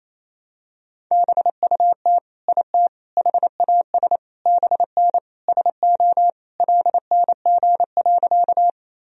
Morse code, heard as a keyed tone, about 28 wpm, 700 Hz. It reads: BUT IT HAS BN SO LNG.
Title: Sentences from Top 100 Words and Abbreviations 28wpm